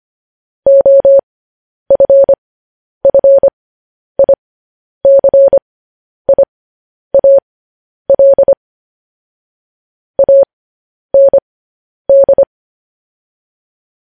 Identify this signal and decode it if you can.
Morse code, heard as a keyed tone, about 25 wpm, spaced as for 10 wpm, 550 Hz. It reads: OFFICIAL AND
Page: news-headlines-in-morse-code-at-10-wpm